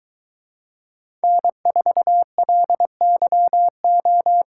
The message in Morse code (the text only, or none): N4LYO